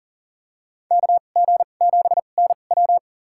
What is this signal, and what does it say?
Morse code, heard as a keyed tone, about 40 wpm, spaced as for 29 wpm, 700 Hz. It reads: KG7NW
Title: USA Call Signs - 5 Characters - 2x Character Spacing - 40wpm